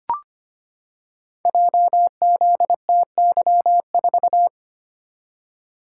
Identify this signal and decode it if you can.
Morse code, heard as a keyed tone, about 25 wpm, 700 Hz. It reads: JZTY4
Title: Code Groups - 5 Letter:Numbers - 25wpm